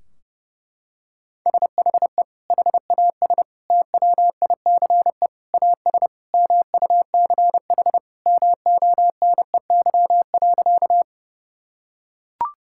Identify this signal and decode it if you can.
Morse code, heard as a keyed tone, about 30 wpm, 700 Hz. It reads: SHE HAS TWICE AS MUCH MONEY.
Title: Sentences from Top 1600 Words 30wpm